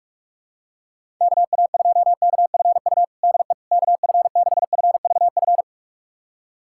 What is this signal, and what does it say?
Morse code, heard as a keyed tone, about 41 wpm, 700 Hz. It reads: KA2KFU DE KF6FVF
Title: USA Callsign DE Callsign - 6 Characters - 45wpm